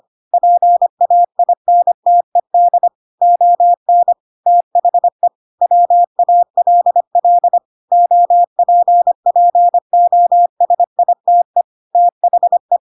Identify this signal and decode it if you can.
Morse code, heard as a keyed tone, about 25 wpm, 700 Hz. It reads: PAINTED ON THE WALL OPPOSITE THE
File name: Pinocchio / Pinocchio ch3 25 25 ik1hge